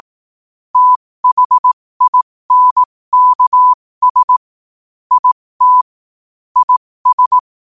Morse code, heard as a keyed tone, about 18 wpm, spaced as for 15 wpm, 1000 Hz. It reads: THINKS IT IS
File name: qotdfiles / 15wpm